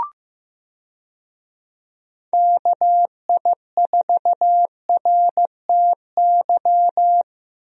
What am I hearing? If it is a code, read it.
Morse code, heard as a keyed tone, about 15 wpm, 700 Hz. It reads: KI4RTY
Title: USA Call Signs - 6 Characters - Repeated 2x - 15wpm